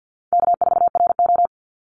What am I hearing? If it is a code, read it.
Morse code, heard as a keyed tone, about 50 wpm, 700 Hz. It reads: K4RO
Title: Easy Call Signs Repeated 2x - 50wpm